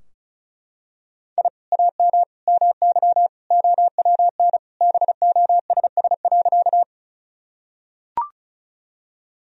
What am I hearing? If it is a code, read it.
Morse code, heard as a keyed tone, about 35 wpm, 700 Hz. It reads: I AM MY OWN BOSS.
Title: Sentences from Top 1000 Words - Review Part 4 - 35wpm